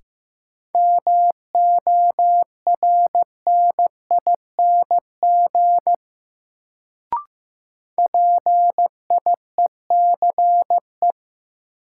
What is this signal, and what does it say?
Morse code, heard as a keyed tone, about 15 wpm, 700 Hz. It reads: MORNING PIECE
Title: Sentences from Top 400 Words - New Word Drill - 15wpm